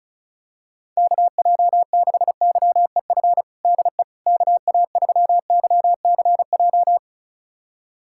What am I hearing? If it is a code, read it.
Morse code, heard as a keyed tone, about 34 wpm, 700 Hz. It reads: KJ6YEF DE KA3YCJ
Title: USA Callsign DE Callsign 35wpm